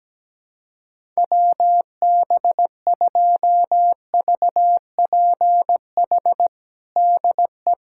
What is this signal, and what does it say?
Morse code, heard as a keyed tone, about 17 wpm, 700 Hz. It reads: WB2VPH DE
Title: USA Callsign DE Callsign 17wpm